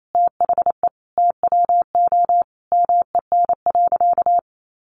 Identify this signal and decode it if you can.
Morse code, heard as a keyed tone, about 28 wpm, 700 Hz. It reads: THE TWO MEN.
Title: Sentences from Top 1300 Words 28wpm